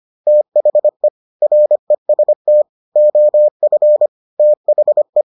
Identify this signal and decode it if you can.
Morse code, heard as a keyed tone, about 25 wpm, 600 Hz. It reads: THE REST OF THE